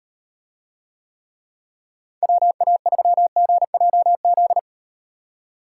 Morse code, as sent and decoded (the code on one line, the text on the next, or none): .-- .- ...-- --. .--- --..
WA3GJZ